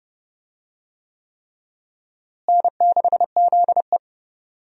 Morse code, heard as a keyed tone, about 30 wpm, 700 Hz. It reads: N6ZE
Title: USA Call Signs - 4 Characters - Repeated 3x - 30wpm